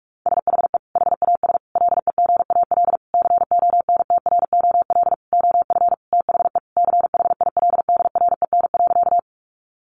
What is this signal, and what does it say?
Morse code, heard as a keyed tone, about 44 wpm, 700 Hz. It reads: SHE HAS LEGAL CONTROL OF THE CHILDREN.
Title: Sentences from Top 1200 Words 45wpm